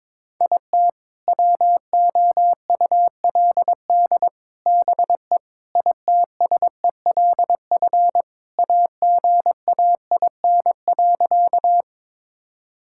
Morse code, heard as a keyed tone, about 21 wpm, 700 Hz. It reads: IT WOULD BE ITSELF AGAIN.
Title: Sentences from Top 1000 Words - Review Part 5 - 22wpm